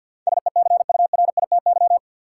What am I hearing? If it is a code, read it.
Morse code, heard as a keyed tone, about 51 wpm, 700 Hz. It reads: SECURITY
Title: Sentences from Top 1000 Words - Review Part 4 - New Word Drill - 50wpm